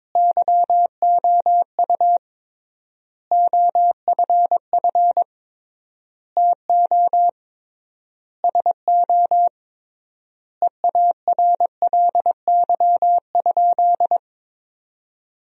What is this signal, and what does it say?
Morse code, heard as a keyed tone, about 22 wpm, 700 Hz. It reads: YOU OFF TO SO EARLY?
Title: 3x Word-Spacing - Sentences from Top 400 Words - 22wpm